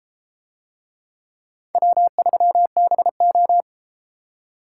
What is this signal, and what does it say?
Morse code, heard as a keyed tone, about 32 wpm, 700 Hz. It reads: W3BO